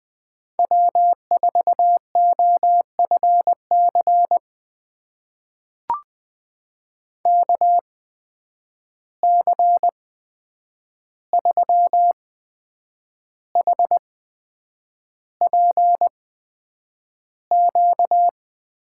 Morse code, heard as a keyed tone, about 20 wpm, spaced as for 8 wpm, 700 Hz. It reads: W4OFC KC3HPQ
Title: USA Call Signs - 8x Character Spacing - 20wpm